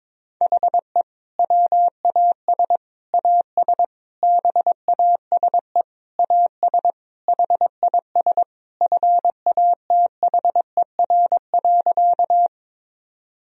Morse code, heard as a keyed tone, about 21 wpm, 700 Hz. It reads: HE WAS AS BASE AS HIS FATHER.
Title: Sentences from Top 1000 Words - Review Part 1 - 22wpm